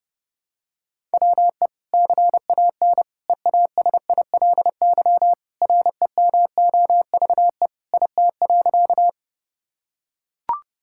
Morse code, heard as a keyed tone, about 30 wpm, 700 Hz. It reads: WE CAN EASILY REMOVE IT.
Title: Sentences from Top 1000 Words - Review Part 5 - 30wpm